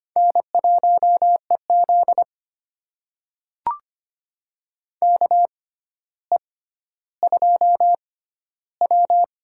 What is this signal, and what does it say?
Morse code, heard as a keyed tone, about 25 wpm, spaced as for 13 wpm, 700 Hz. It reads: N1EZ KE2W
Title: USA Call Signs - 4 Characters - 6x Character Spacing - 25wpm